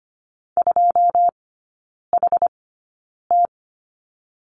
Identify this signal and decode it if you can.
Morse code, heard as a keyed tone, about 25 wpm, spaced as for 9 wpm, 700 Hz. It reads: 2HT
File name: course / 25:10wpm / 093 Lesson 16e - Letter c - Three Characters 25wpm